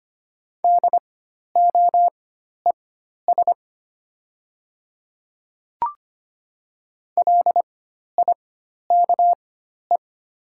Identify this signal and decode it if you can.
Morse code, heard as a keyed tone, about 25 wpm, spaced as for 12 wpm, 690 Hz. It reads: DOES LIKE
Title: 4x Character Spacing - Top 500 Words - 25wpm